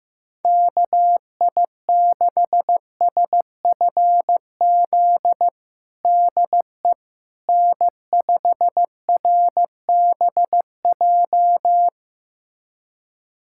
Morse code, heard as a keyed tone, about 15 wpm, 700 Hz. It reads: KI6SFZ DE N5RBJ
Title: USA Callsign DE Callsign 15wpm